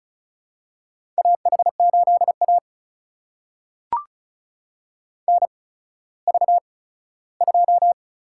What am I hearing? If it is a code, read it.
Morse code, heard as a keyed tone, about 36 wpm, spaced as for 17 wpm, 700 Hz. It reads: AH8A NV2
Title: USA Call Signs - 4 Characters - 8x Character Spacing - 35wpm